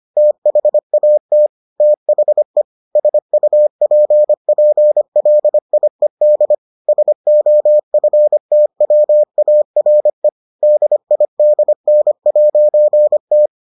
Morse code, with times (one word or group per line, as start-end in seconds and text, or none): 0.2-1.5 THAT
1.8-2.6 THE
3.0-6.5 SUPPLIED
6.9-10.3 SOFTWARE
10.6-13.5 DIDN'T